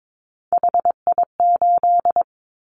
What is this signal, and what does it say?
Morse code, heard as a keyed tone, about 22 wpm, 700 Hz. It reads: HI8
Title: Callsign Prefixes for Top 100 DXCC - 22wpm